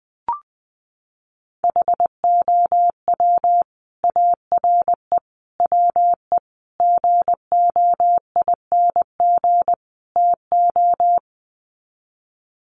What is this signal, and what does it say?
Morse code, heard as a keyed tone, about 20 wpm, 700 Hz. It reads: HOW ARE WE GOING TO
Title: Sets of 5 Words Encore 20wpm